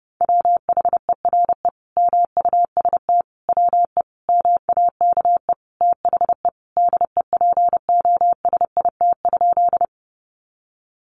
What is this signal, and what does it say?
Morse code, heard as a keyed tone, about 30 wpm, 700 Hz. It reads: WHERE MUST WE MAKE THE DEPOSIT?
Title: Sentences from Top 1300 Words 30wpm